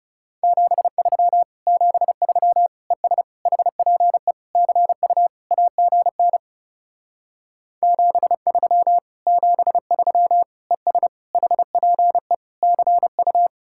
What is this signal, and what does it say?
Morse code, heard as a keyed tone, about 39 wpm, 700 Hz. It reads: 73 73 ES HPE CU AGN 73 73 ES HPE CU